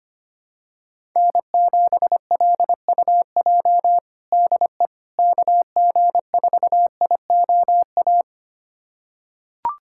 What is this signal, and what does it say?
Morse code, heard as a keyed tone, about 25 wpm, 700 Hz. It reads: N7LUJ DE KG4IOA
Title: USA Callsign DE Callsign 25wpm